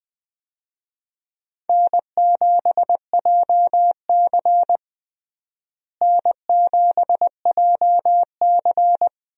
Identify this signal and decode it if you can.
Morse code, heard as a keyed tone, about 20 wpm, 700 Hz. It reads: N7JC N7JC
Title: USA Call Signs - 4 Characters - Repeated 2x - 20wpm